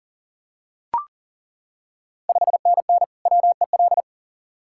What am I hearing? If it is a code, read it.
Morse code, heard as a keyed tone, about 40 wpm, 690 Hz. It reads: 5NN WEL
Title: Ontario QSO Party Exchange 40wpm